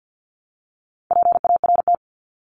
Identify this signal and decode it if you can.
Morse code, heard as a keyed tone, about 50 wpm, 700 Hz. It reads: PART
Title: LY Suffix - Top 100 Words with Base Words 50wpm